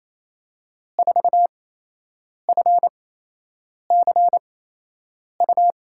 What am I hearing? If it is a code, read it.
Morse code, heard as a keyed tone, about 28 wpm, spaced as for 8 wpm, 700 Hz. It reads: 4FCU
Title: USA Call Signs - 6 Characters - 8x Character Spacing - 28wpm